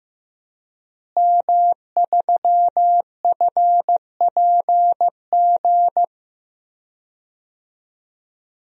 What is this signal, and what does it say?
Morse code, heard as a keyed tone, about 15 wpm, 700 Hz. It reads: M3FPG
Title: DXCC-100 Call Signs - 5 Characters - Repeated 3x - 15wpm